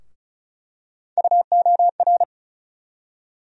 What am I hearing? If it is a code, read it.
Morse code, heard as a keyed tone, about 36 wpm, 700 Hz. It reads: UOR
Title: USA Call Sign Postfixes - 3 Letters 35wpm